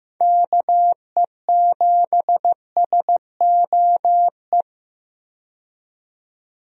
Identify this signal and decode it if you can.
Morse code, heard as a keyed tone, about 15 wpm, 700 Hz. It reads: KE7SOE
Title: USA Call Signs - 6 Characters - 15wpm